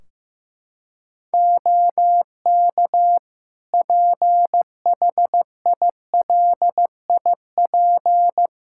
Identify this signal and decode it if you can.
Morse code, heard as a keyed tone, about 15 wpm, 700 Hz. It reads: OK PHILIP
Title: Practice QSO - Exchange 4 - Ending - 15wpm